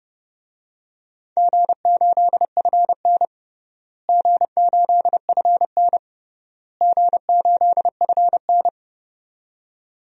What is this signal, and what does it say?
Morse code, heard as a keyed tone, about 30 wpm, 700 Hz. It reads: G8FN G8FN G8FN
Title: DXCC-100 Call Signs - 4 Characters - Repeated 3x - 30wpm